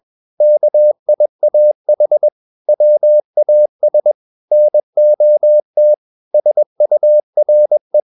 Morse code, heard as a keyed tone, about 21 wpm, 600 Hz. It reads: KIAH WAS NOT SURE
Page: CW med Gnister